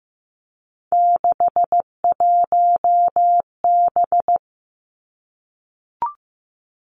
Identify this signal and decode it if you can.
Morse code, heard as a keyed tone, about 15 wpm, 700 Hz. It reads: 61B
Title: Three Challenging Letter-Number Combinations 15wpm